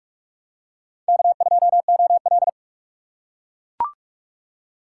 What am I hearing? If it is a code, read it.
Morse code, heard as a keyed tone, about 44 wpm, 700 Hz. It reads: K2OL